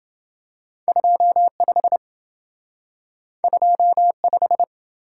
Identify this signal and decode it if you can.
Morse code, heard as a keyed tone, about 27 wpm, 700 Hz. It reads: 25 25